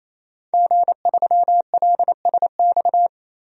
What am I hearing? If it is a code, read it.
Morse code, heard as a keyed tone, about 28 wpm, 700 Hz. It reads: G3LSX
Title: DXCC-100 Call Signs - 5 Characters - 28wpm